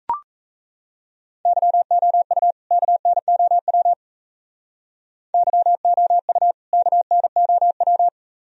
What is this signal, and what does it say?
Morse code, heard as a keyed tone, about 34 wpm, 700 Hz. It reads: YOU KNOW YOU KNOW
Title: Speed-Racing - Sets of 2 Words 28wpm